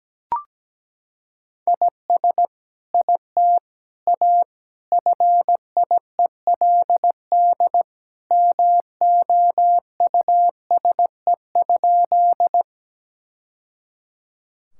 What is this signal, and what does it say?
Morse code, heard as a keyed tone, about 17 wpm, 700 Hz. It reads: IS IT A FIELD MOUSE?
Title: Sentences from Top 1000 Words - Review Part 5 - 17wpm